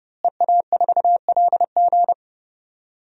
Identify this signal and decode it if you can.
Morse code, heard as a keyed tone, about 32 wpm, 700 Hz. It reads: EA4LG
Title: DXCC-100 Call Signs - 5 Characters - Repeated 3x - 30wpm